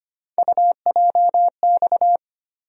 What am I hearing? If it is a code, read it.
Morse code, heard as a keyed tone, about 25 wpm, 700 Hz. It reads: UJX
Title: USA Call Sign Postfixes - 3 Letters 25wpm